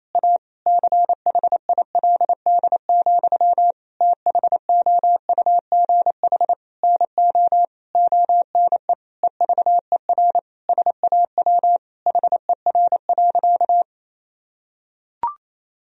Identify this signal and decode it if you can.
Morse code, heard as a keyed tone, about 27 wpm, 700 Hz. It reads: A CHILD, THOUGH NO ONE EVER SAW HER.
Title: Sentences from Top 1000 Words - Review Part 2 - 28wpm